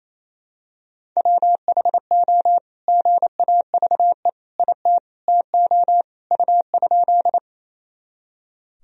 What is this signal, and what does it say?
Morse code, heard as a keyed tone, about 28 wpm, 700 Hz. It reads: WHO GAVE IT TO U?
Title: Sentences from Top 500 Words and Abbreviations 28wpm